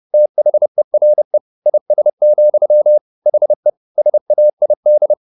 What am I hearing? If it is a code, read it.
Morse code, heard as a keyed tone, about 30 wpm, 600 Hz. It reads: THERE IS, HE SAID